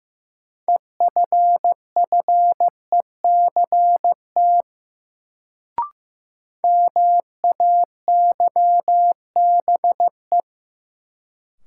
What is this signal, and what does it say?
Morse code, heard as a keyed tone, about 15 wpm, 700 Hz. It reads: EFFECT MAYBE
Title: Sentences from Top 1000 Words - New Word Drill - 15wpm